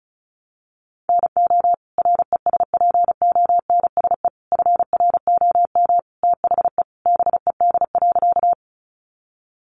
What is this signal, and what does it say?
Morse code, heard as a keyed tone, about 34 wpm, 700 Hz. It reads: NO RESPONSE FROM THE BED.
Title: Sentences from Top 1400 Words 35wpm